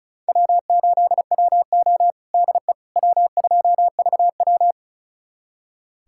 Morse code, heard as a keyed tone, about 36 wpm, 700 Hz. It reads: W8WO DE W2VW
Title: USA Callsign DE Callsign - 4 Characters - 35wpm